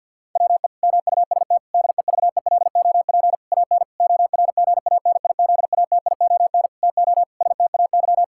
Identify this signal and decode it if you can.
Morse code, heard as a keyed tone, about 50 wpm, 700 Hz. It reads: WE MUST DEVELOP AN ORGANIZATION TO STAY
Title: Sentences from Top 1000 Words - Review Part 5 - 50wpm